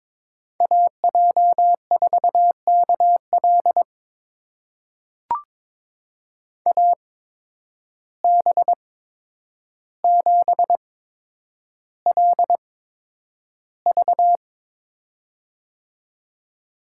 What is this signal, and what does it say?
Morse code, heard as a keyed tone, about 22 wpm, spaced as for 9 wpm, 700 Hz. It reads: AJ4KL AB7LV
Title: USA Call Signs - 5 Characters - 8x Character Spacing - 22wpm